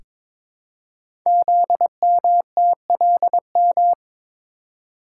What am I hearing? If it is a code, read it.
Morse code, heard as a keyed tone, about 21 wpm, 700 Hz. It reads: ZMTLM
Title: Code Groups - 5 Letters - 22wpm